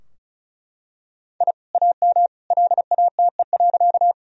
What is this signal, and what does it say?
Morse code, heard as a keyed tone, about 34 wpm, 700 Hz. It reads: I AM LATE.
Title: Sentences from Top 1000 Words - Review Part 2 - 35wpm